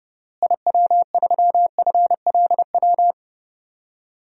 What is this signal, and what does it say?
Morse code, heard as a keyed tone, about 30 wpm, 700 Hz. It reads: IW3FLW
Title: DXCC-100 Call Signs - 6 Characters - Repeated 3x - 30wpm